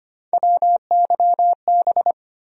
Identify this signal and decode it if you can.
Morse code, heard as a keyed tone, about 25 wpm, 700 Hz. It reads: WYB
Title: USA Call Sign Postfixes - 3 Letters 25wpm